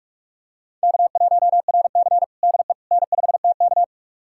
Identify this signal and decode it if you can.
Morse code, heard as a keyed tone, about 45 wpm, 700 Hz. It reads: K1RC DE N5TK
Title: USA Callsign DE Callsign - 4 Characters - 45wpm